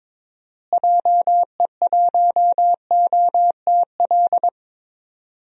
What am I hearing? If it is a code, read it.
Morse code, heard as a keyed tone, about 22 wpm, 700 Hz. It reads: JE1OTL
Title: DXCC-100 Call Signs - 6 Characters - Repeated 2x - 22wpm